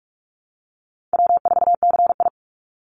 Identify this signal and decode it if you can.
Morse code, heard as a keyed tone, about 44 wpm, 700 Hz. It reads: W4CI
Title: Easy Call Signs Repeated 2x - 45wpm